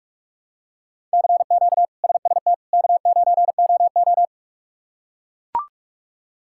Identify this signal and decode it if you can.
Morse code, heard as a keyed tone, about 41 wpm, 700 Hz. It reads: CQ SST K9OO
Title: K1USN SST QSO Practice - 45wpm